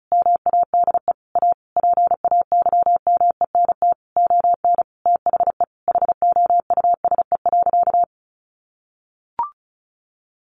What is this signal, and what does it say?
Morse code, heard as a keyed tone, about 34 wpm, 700 Hz. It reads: MADE A PAYMENT ON THE HOUSE.
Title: Sentences from Top 1400 Words 35wpm